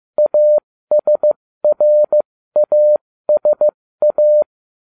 Morse code, heard as a keyed tone, about 15 wpm, spaced as for 13 wpm, 600 Hz. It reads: ASRASA